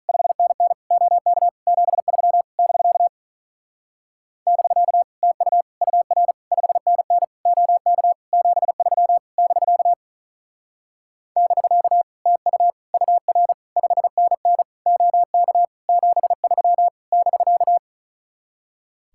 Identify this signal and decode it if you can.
Morse code, heard as a keyed tone, about 44 wpm, 700 Hz. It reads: 5NN OK 73 <BK> <BK> TU UR 5NN OK 73 <BK> <BK> TU UR 5NN OK 73 <BK>